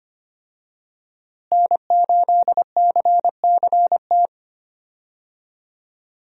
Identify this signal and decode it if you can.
Morse code, heard as a keyed tone, about 25 wpm, 700 Hz. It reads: N8CCT
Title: USA Call Signs - 5 Characters - Repeated 3x - 25wpm